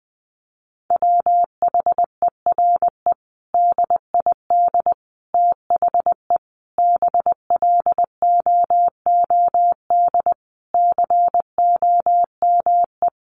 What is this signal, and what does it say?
Morse code, heard as a keyed tone, about 20 wpm, 700 Hz. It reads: WHERE DID THE BLOOD COME
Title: Sentences from Top 1100 Words 20wpm